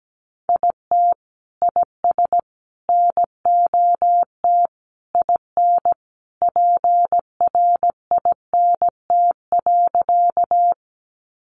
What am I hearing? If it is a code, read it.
Morse code, heard as a keyed tone, about 17 wpm, 700 Hz. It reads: IT IS NOT IN PRINT.
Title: Sentences from Top 1400 Words 17wpm